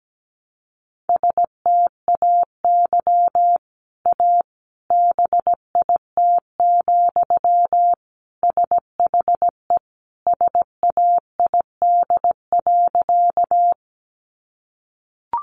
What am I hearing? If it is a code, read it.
Morse code, heard as a keyed tone, about 17 wpm, 700 Hz. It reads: STAY A BIT, SHE SAID.
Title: Sentences from Top 900 Words 17wpm